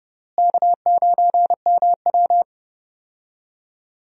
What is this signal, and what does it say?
Morse code, heard as a keyed tone, about 30 wpm, 700 Hz. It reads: K9MW
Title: USA Call Signs - 4 Characters - Repeated 2x - 30wpm